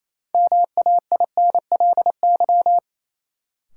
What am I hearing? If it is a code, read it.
Morse code, heard as a keyed tone, about 28 wpm, 700 Hz. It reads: MAINLY
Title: Sentences from Top 1400 Words - New Word Drill - 28wpm